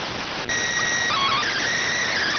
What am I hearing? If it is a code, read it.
SSTV (slow-scan television), PD 290.